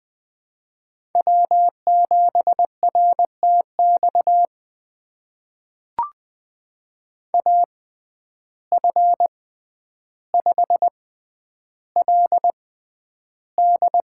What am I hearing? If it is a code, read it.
Morse code, heard as a keyed tone, about 20 wpm, spaced as for 10 wpm, 700 Hz. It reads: W7RTX AF5LD
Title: USA Call Signs - 5 Characters - 6x Character Spacing - 20wpm